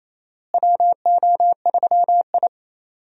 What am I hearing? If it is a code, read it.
Morse code, heard as a keyed tone, about 28 wpm, 700 Hz. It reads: WO3I
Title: USA Call Signs - 4 Characters - Repeated 3x - 28wpm